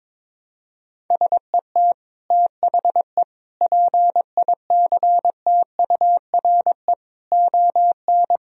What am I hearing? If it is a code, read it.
Morse code, heard as a keyed tone, about 21 wpm, 700 Hz. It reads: SET THE PICTURE ON
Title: Sentences from Top 1000 Words - Review Part 1 - 22wpm